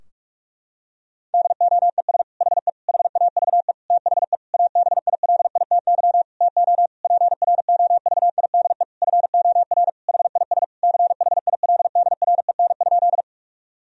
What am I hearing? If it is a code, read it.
Morse code, heard as a keyed tone, about 46 wpm, 700 Hz. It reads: DOES HE HAVE THE ABILITY TO PROVIDE FOR HIS CHILDREN?
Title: Sentences from Top 1000 Words - Review Part 3 - 45wpm